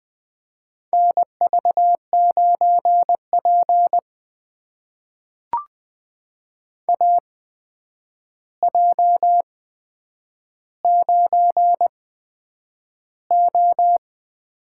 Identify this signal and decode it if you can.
Morse code, heard as a keyed tone, about 20 wpm, spaced as for 9 wpm, 700 Hz. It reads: NV9P AJ9O
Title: USA Call Signs - 4 Characters - 8x Character Spacing - 20wpm